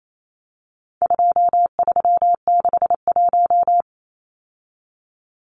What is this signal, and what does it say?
Morse code, heard as a keyed tone, about 28 wpm, 700 Hz. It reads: 2361
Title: Numbers - 4-Digit 28wpm